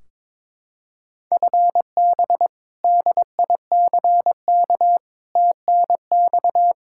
Morse code, heard as a keyed tone, about 22 wpm, 700 Hz. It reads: FB DICK TNX
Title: Practice QSO - Exchange 3 - More Info - 22wpm